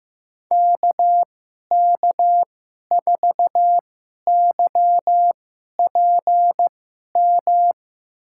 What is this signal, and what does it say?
Morse code, heard as a keyed tone, about 15 wpm, spaced as for 11 wpm, 700 Hz. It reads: KK4YPM
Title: USA Call Signs - 6 Characters - 2x Character Spacing - 15wpm